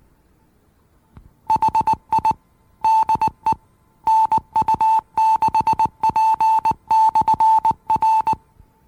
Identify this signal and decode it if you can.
Morse code, heard as a keyed tone, about 19 wpm, 880 Hz. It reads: HI DE NU6P/R
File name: acc / acc-rc-850-rc-96-and-rc-85-plus-dvr-demos